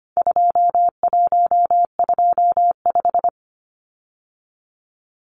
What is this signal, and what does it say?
Morse code, heard as a keyed tone, about 25 wpm, 700 Hz. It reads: 2125
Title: Numbers - 4-Digit 25wpm